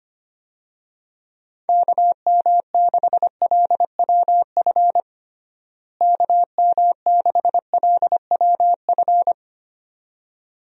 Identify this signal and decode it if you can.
Morse code, heard as a keyed tone, about 25 wpm, 700 Hz. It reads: KM6LWF KM6LWF
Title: USA Call Signs - 6 Characters - Repeated 2x - 25wpm